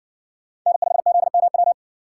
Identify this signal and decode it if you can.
Morse code, heard as a keyed tone, about 60 wpm, 700 Hz. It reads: N5ZMW